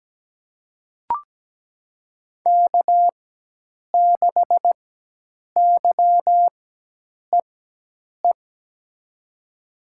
Morse code, heard as a keyed tone, about 17 wpm, spaced as for 8 wpm, 700 Hz. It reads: K6YEE
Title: USA Call Signs - 5 Characters - 4x Character Spacing - 17wpm